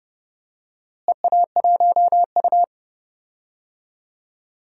Morse code, heard as a keyed tone, about 32 wpm, 700 Hz. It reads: EA1U
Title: DXCC-100 Call Signs - 4 Characters - Repeated 2x - 30wpm